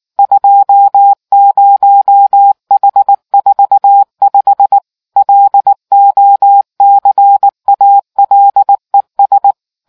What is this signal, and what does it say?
Morse code, heard as a keyed tone, about 19 wpm, 800 Hz. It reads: 20H45 LOCALES